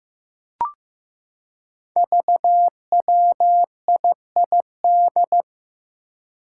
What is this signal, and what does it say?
Morse code, heard as a keyed tone, about 15 wpm, 700 Hz. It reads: VWIID
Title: Five Challenging Letter-Number Combinations 15wpm